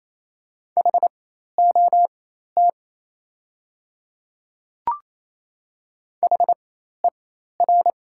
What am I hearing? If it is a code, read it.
Morse code, heard as a keyed tone, about 28 wpm, spaced as for 13 wpm, 700 Hz. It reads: HOT HER
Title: 4x Character Spacing - Top 100 Words - 28wpm